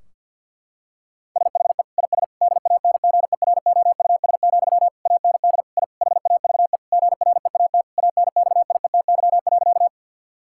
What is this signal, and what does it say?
Morse code, heard as a keyed tone, about 49 wpm, 700 Hz. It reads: SHE IS DANGEROUS, AND I HAVE GREAT ANXIETY.
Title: Sentences from Top 1600 Words 50wpm